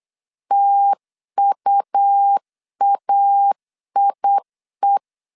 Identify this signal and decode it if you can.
Morse code, heard as a keyed tone, about 8 wpm, 800 Hz. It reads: TUAIE
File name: CW / DET / 1 3 1500C S B 8wpm 4141